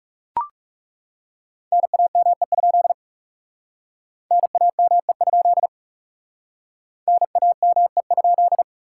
Morse code, heard as a keyed tone, about 39 wpm, 700 Hz. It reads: NAME? NAME? NAME?